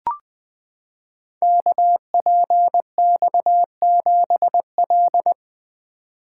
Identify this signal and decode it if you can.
Morse code, heard as a keyed tone, about 20 wpm, 700 Hz. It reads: KPX7L
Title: Code Groups - 5 Letter:Numbers - 20wpm